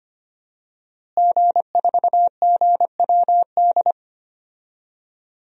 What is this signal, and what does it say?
Morse code, heard as a keyed tone, about 25 wpm, 700 Hz. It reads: G4GWD
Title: DXCC-100 Call Signs - 5 Characters - Repeated 2x - 25wpm